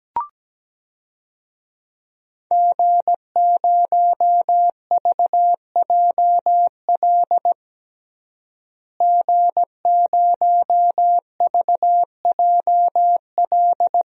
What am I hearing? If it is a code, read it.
Morse code, heard as a keyed tone, about 17 wpm, 700 Hz. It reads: G0VJL G0VJL
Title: DXCC-100 Call Signs - 5 Characters - Repeated 3x - 17wpm